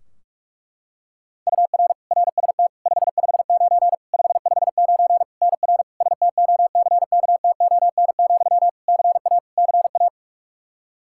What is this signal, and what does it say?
Morse code, heard as a keyed tone, about 46 wpm, 700 Hz. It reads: UR RST 559 559 NR STOCKTON, CA CA
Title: Practice QSO - Exchange 1 - Beginning - 45wpm